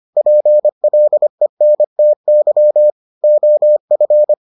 Morse code, heard as a keyed tone, about 25 wpm, 600 Hz. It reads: PLENTY OF